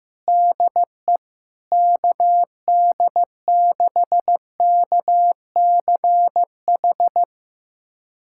Morse code, heard as a keyed tone, about 15 wpm, 700 Hz. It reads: DE KD6KCH
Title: USA Callsign DE Callsign - 6 Characters - 15wpm